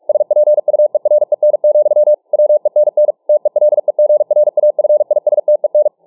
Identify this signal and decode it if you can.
Morse code, heard as a keyed tone, about 44 wpm, 600 Hz. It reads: SPUEREN, WENN TELEGRAFISTEN